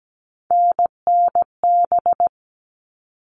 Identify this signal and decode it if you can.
Morse code, heard as a keyed tone, about 17 wpm, 700 Hz. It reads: NNB
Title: All Cut Numbers - 3-Digit 17wpm